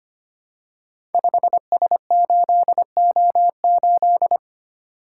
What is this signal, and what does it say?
Morse code, heard as a keyed tone, about 25 wpm, 700 Hz. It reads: 5S8O8
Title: Code Groups - 5 Letter:Numbers - 25wpm